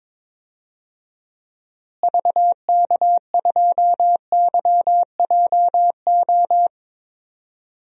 Morse code, heard as a keyed tone, about 22 wpm, 700 Hz. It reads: VK2YJO
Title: DXCC-100 Call Signs - 6 Characters - Repeated 2x - 22wpm